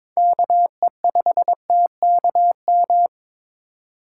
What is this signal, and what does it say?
Morse code, heard as a keyed tone, about 22 wpm, 700 Hz. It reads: KE5TKM
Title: USA Call Signs - 6 Characters - 2x Character Spacing - 22wpm